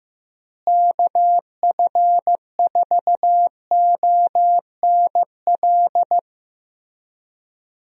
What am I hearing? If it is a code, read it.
Morse code, heard as a keyed tone, about 15 wpm, 700 Hz. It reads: KF4ONL
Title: USA Call Signs - 6 Characters - Repeated 2x - 15wpm